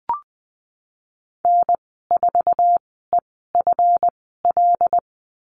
Morse code, heard as a keyed tone, about 19 wpm, spaced as for 14 wpm, 700 Hz. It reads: N4EFL
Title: 154 Lesson 25g - Letter p - Call Signs 20wpm